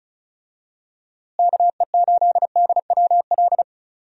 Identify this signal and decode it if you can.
Morse code, heard as a keyed tone, about 36 wpm, 700 Hz. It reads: KE8DWL
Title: USA Call Signs - 6 Characters - Repeated 3x - 35wpm